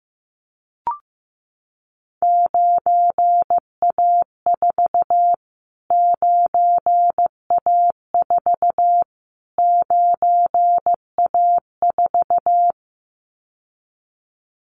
Morse code, heard as a keyed tone, about 15 wpm, 700 Hz. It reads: 9A4 9A4 9A4
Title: Callsign Prefixes for Top 25 DXCC - Repeated 3x - 15wpm